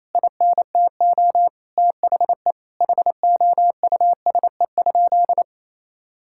Morse code, heard as a keyed tone, about 28 wpm, 700 Hz. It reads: INTO THE HOUSE?
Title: Sentences from Top 1000 Words - Review Part 5 - 28wpm